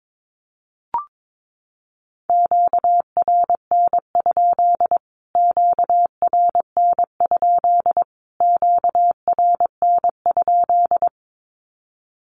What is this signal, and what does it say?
Morse code, heard as a keyed tone, about 22 wpm, 700 Hz. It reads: QRN? QRN? QRN?
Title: Q-Codes Repeated 3x 22wpm